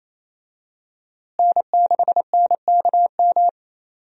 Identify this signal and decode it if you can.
Morse code, heard as a keyed tone, about 28 wpm, 700 Hz. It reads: N6NKM
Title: USA Call Signs - 5 Characters - Repeated 2x - 28wpm